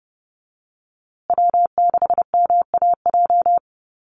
Morse code, heard as a keyed tone, about 30 wpm, 700 Hz. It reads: W6MAJ
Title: USA Call Signs Repeated 3x 30wpm